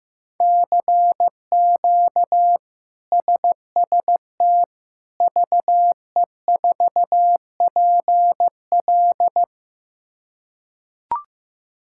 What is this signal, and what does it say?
Morse code, heard as a keyed tone, about 15 wpm, 700 Hz. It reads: CQ SST VE4PL
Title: K1USN SST QSO Practice - 15wpm